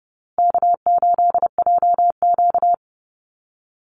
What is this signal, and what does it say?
Morse code, heard as a keyed tone, about 30 wpm, 700 Hz. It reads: K8JQ
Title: Easy Call Signs Repeated 3x - 30wpm